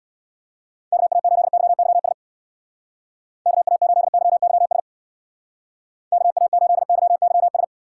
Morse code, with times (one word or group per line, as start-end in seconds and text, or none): 0.9-2.1 KA8YYS
3.5-4.8 KA8YYS
6.1-7.6 KA8YYS